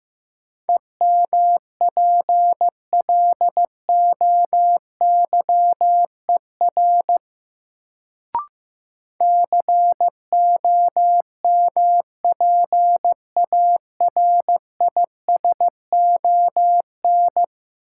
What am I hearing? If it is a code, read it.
Morse code, heard as a keyed tone, about 15 wpm, 700 Hz. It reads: EMPLOYER COMPARISON
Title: Sentences from Top 1600 Words - New Word Drill Alt - 15wpm